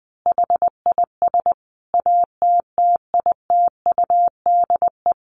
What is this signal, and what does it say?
Morse code, heard as a keyed tone, about 20 wpm, 700 Hz. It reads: HIS ATTITUDE
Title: Sentences from Top 1300 Words 20wpm